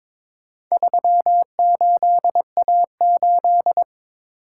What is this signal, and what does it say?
Morse code, heard as a keyed tone, about 22 wpm, 690 Hz. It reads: 38A8
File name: practice / 22wpm / Four Letter-Number Combinations - 22wpm